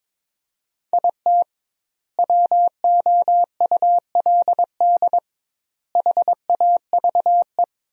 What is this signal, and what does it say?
Morse code, heard as a keyed tone, about 22 wpm, 700 Hz. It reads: IT WOULD HAVE
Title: 2x Word-Spacing - Sentences from Top 400 Words - 22wpm